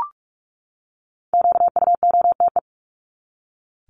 Morse code, heard as a keyed tone, about 44 wpm, 700 Hz. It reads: QUOTE
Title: Top 1751-2000 Words 45wpm